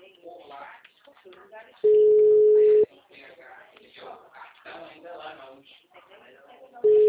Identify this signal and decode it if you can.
Ringback tone (Europe, 425 Hz), with speech.